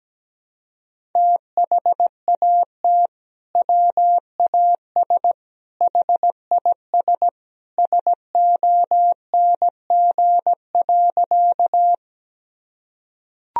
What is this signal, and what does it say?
Morse code, heard as a keyed tone, about 17 wpm, 700 Hz. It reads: THAT WAS HIS SONG.
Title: Sentences from Top 1000 Words - Review Part 2 - 17wpm